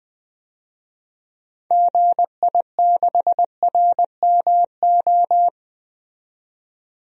Morse code, heard as a keyed tone, about 20 wpm, 700 Hz. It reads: GI6RMO
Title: DXCC-100 Call Signs - 6 Characters - 20wpm